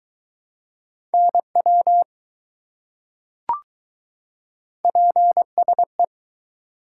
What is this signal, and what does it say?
Morse code, heard as a keyed tone, about 20 wpm, 700 Hz. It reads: NW PSE